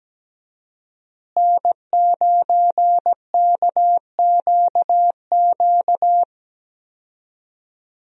Morse code, heard as a keyed tone, about 17 wpm, 700 Hz. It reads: N9KQQ
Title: USA Call Signs - 5 Characters - Repeated 2x - 17wpm